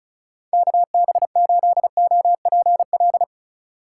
Morse code, heard as a keyed tone, about 34 wpm, 700 Hz. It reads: KB8OPL
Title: USA Call Signs - 6 Characters - 8x Character Spacing - 35wpm